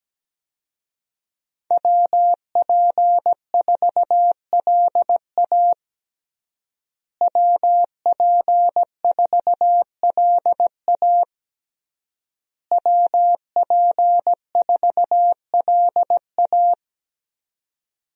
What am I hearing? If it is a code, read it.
Morse code, heard as a keyed tone, about 17 wpm, 700 Hz. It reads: WP4LA WP4LA WP4LA
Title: USA Call Signs - 5 Characters - Repeated 3x - 17wpm